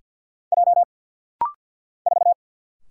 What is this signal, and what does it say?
Morse code, heard as a keyed tone, about 50 wpm, 690 Hz. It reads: J 4
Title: Single Letter-Number Alt - Rapid-Fire 50wpm